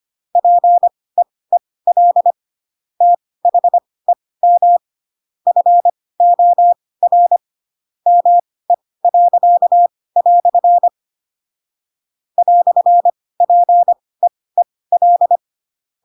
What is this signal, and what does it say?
Morse code, heard as a keyed tone, about 25 wpm, spaced as for 18 wpm, 700 Hz. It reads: PEEL THEM FOR ME." "PEEL
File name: Pinocchio / Pinocchio ch7 12 25 ik1hge